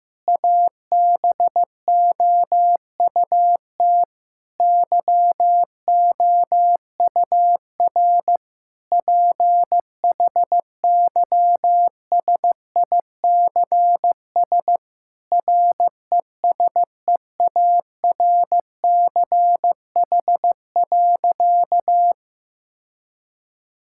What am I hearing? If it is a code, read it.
Morse code, heard as a keyed tone, about 16 wpm, 700 Hz. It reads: ABOUT YOUR PHYSICS RESEARCH.
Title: Sentences from Top 1000 Words - Review Part 5 - 15wpm